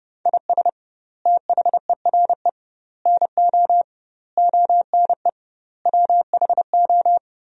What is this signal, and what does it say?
Morse code, heard as a keyed tone, about 32 wpm, 700 Hz. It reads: IS THERE NO ONE WHO
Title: 2x Word-Spacing - Sentences from Top 200 Words - 30wpm